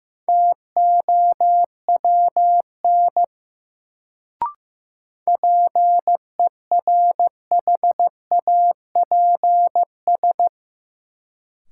Morse code, heard as a keyed tone, about 15 wpm, 700 Hz. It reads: TOWN PERHAPS
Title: Sentences from Top 500 Words - New Word Drill - 15wpm